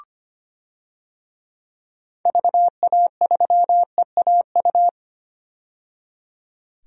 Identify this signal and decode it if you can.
Morse code, heard as a keyed tone, about 25 wpm, 700 Hz. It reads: VA3EAU
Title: DXCC-100 Call Signs - 6 Characters - 25wpm